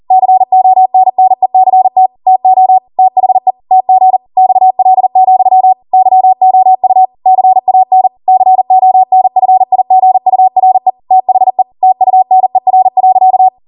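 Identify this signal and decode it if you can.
Morse code, heard as a keyed tone, about 40 wpm, 750 Hz. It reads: CONNECT TO THE TG XL, YOU CAN CONFIGURE THE TUNER.
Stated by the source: Code practice at 40 wpm.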